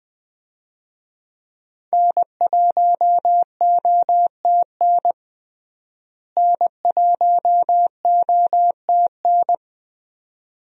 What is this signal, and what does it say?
Morse code, heard as a keyed tone, about 20 wpm, 700 Hz. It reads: N1OTN N1OTN
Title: USA Call Signs - 5 Characters - Repeated 2x - 20wpm